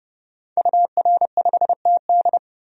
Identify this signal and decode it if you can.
Morse code, heard as a keyed tone, about 30 wpm, 700 Hz. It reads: UR5TD
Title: DXCC-100 Call Signs - 5 Characters - 30wpm